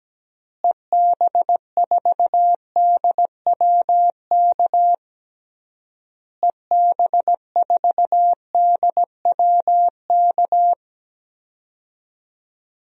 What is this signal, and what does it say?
Morse code, heard as a keyed tone, about 17 wpm, 700 Hz. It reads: EB4DWK EB4DWK
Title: DXCC-100 Call Signs - 6 Characters - Repeated 3x - 17wpm